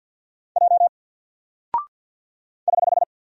Morse code, heard as a keyed tone, about 48 wpm, 700 Hz. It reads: J <HH>
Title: Single Character (Letter:Number:Punctuation:Prosign) 50wpm